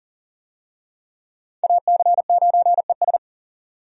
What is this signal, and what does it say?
Morse code, heard as a keyed tone, about 40 wpm, 700 Hz. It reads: AC9ES
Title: USA Call Signs - 5 Characters - 40wpm